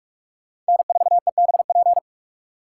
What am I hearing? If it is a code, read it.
Morse code, heard as a keyed tone, about 46 wpm, 700 Hz. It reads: N4EBP
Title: USA Call Signs - 4x Character Spacing - 45wpm